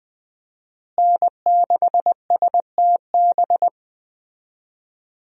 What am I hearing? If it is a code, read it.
Morse code, heard as a keyed tone, about 20 wpm, 700 Hz. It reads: N6STB
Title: USA Call Signs - 5 Characters - Repeated 2x - 20wpm